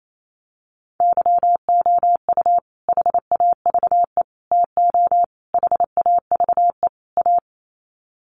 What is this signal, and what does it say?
Morse code, heard as a keyed tone, about 28 wpm, 700 Hz. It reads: YOU HAVE TO HAVE A
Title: Sets of 5 Words Encore 28wpm